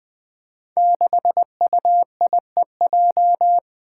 Morse code, heard as a keyed tone, about 20 wpm, 700 Hz. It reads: 6UIEJ
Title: Five Challenging Letter-Number Combinations 20wpm